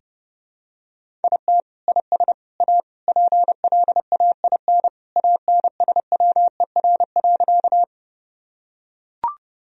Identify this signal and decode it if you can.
Morse code, heard as a keyed tone, about 32 wpm, 700 Hz. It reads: IT IS A PLAIN ANSWER.
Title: Sentences from Top 1000 Words - Review Part 2 - 30wpm